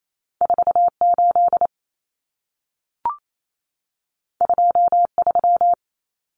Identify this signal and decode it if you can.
Morse code, heard as a keyed tone, about 28 wpm, 700 Hz. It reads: 48 23
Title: Common Cut Numbers - 2-Digit 28wpm